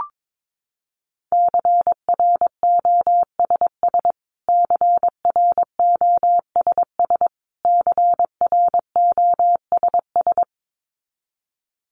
Morse code, heard as a keyed tone, about 22 wpm, 700 Hz. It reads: CROSS CROSS CROSS
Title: Top 500 Words Repeated 22wpm